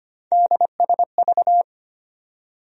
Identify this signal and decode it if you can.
Morse code, heard as a keyed tone, about 25 wpm, 700 Hz. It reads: DSV
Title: USA Call Sign Postfixes - 3 Letters 25wpm